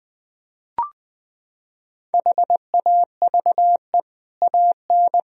Morse code, heard as a keyed tone, about 21 wpm, 700 Hz. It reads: HAVE AN
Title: Sentences from Top 1000 Words - Review Part 1 - 20wpm